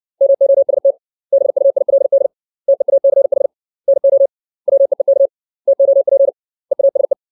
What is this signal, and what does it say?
Morse code, heard as a keyed tone, about 60 wpm, 550 Hz. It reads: MOST BUILD TEACH NO WEEK TOP EASE
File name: 632 words / 632 Words 60WPM